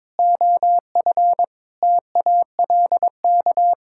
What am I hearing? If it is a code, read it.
Morse code, heard as a keyed tone, about 22 wpm, 700 Hz. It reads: OF TALK
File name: practice / 22wpm / Sets of 7 Words - 22wpm